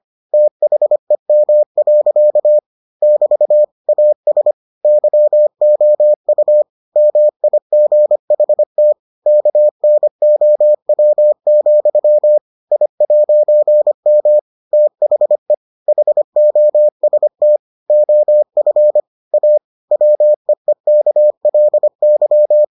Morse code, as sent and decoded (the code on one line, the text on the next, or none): - .... -- .-.-.- / -...- / .- ... / -.-- --- ..- / -- .. --. .... - / -.- -. --- .-- --..-- / .. .----. -- / - .... / .... --- ... - / --- ..-. / .- / .-- . -.- .-.. -.--
THEM. <BT> AS YOU MIGHT KNOW, I'M THE HOST OF A WEEKLY